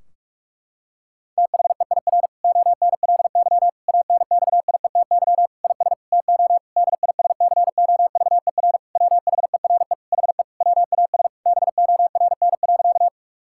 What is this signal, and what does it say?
Morse code, heard as a keyed tone, about 46 wpm, 700 Hz. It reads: THEIR ONLY ANXIETY IS TO DISCOVER WHERE HE WAS BORN.
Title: Sentences from Top 1600 Words 45wpm